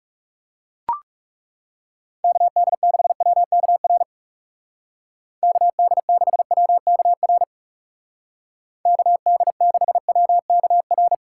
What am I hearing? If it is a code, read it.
Morse code, heard as a keyed tone, about 40 wpm, 700 Hz. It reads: KD6WKR KD6WKR KD6WKR